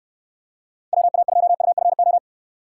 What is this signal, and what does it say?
Morse code, heard as a keyed tone, about 69 wpm, 700 Hz. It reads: WA2UFO